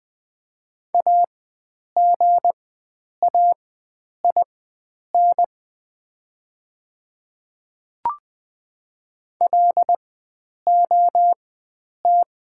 Morse code, heard as a keyed tone, about 20 wpm, spaced as for 9 wpm, 700 Hz. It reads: AGAIN LOT
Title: 4x Character Spacing - Top 500 Words - 20wpm